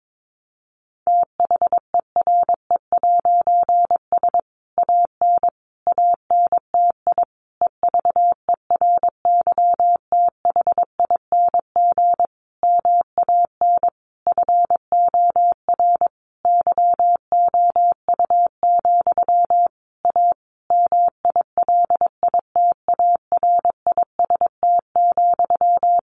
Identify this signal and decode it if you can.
Morse code, heard as a keyed tone, about 22 wpm, 700 Hz. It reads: THERE'S AN ANTI EVERYTHING MAN FOR YOU, A MILITARIST,
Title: A Sound Of Thunder 22wpm